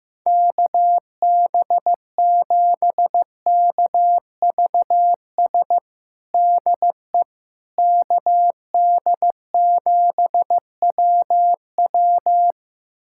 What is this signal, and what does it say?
Morse code, heard as a keyed tone, about 15 wpm, 700 Hz. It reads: KB7KVS DE KD7WW
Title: USA Callsign DE Callsign 15wpm